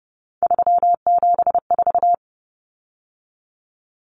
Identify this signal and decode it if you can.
Morse code, heard as a keyed tone, about 30 wpm, 700 Hz. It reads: 374
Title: Numbers - 3-Digit 30wpm